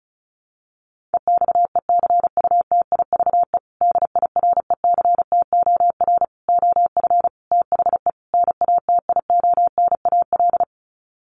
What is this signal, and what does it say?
Morse code, heard as a keyed tone, about 34 wpm, 700 Hz. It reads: EXECUTIVE DIRECTOR OF THE NATIONAL
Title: Sets of 5 Words Encore2 35wpm